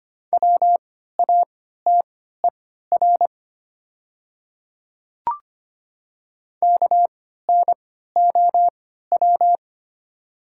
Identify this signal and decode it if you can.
Morse code, heard as a keyed tone, about 25 wpm, spaced as for 14 wpm, 700 Hz. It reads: WATER KNOW
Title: 3x Character Spacing - Top 200 Words - 25wpm